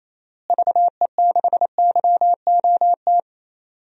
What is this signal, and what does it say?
Morse code, heard as a keyed tone, about 28 wpm, 700 Hz. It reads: VE6YOT
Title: DXCC-100 Call Signs - 6 Characters - Repeated 3x - 28wpm